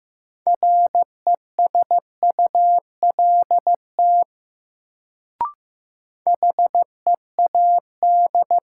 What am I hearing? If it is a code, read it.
Morse code, heard as a keyed tone, about 15 wpm, 700 Hz. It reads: RESULT HEAD
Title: Sets of 3 Words - New Word Drill - 15wpm